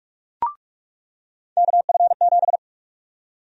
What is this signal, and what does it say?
Morse code, heard as a keyed tone, about 46 wpm, 700 Hz. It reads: KF7
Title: USA Call Sign Prefixes - 45wpm